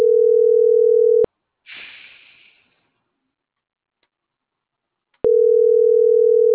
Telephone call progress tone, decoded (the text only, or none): ringback tone